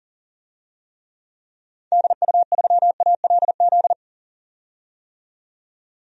Morse code, heard as a keyed tone, about 38 wpm, 700 Hz. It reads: DU3ALZ
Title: DXCC-100 Call Signs - 6 Characters - 40wpm